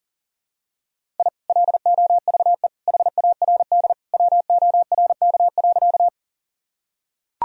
Morse code, heard as a keyed tone, about 37 wpm, 700 Hz. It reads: I LOVE HARD WORK.
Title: Sentences from Top 1000 Words - Review Part 1 - 40wpm